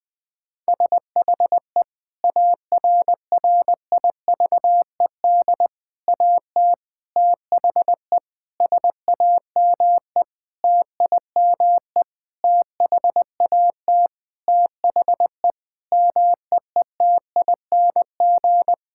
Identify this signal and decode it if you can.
Morse code, heard as a keyed tone, about 21 wpm, 700 Hz. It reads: SHE ARRIVED AT THE SAME TIME THAT THE MEETING
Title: Sentences with 5-Word Phrases - 20wpm